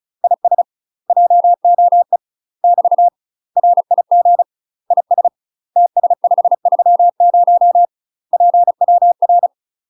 Morse code, heard as a keyed tone, about 36 wpm, 700 Hz. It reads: IS JOE <BT> RIG IS TS530 PWR
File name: UK MASTER QSOs TRAINING / 35WPM / UK MASTER QSOs 0010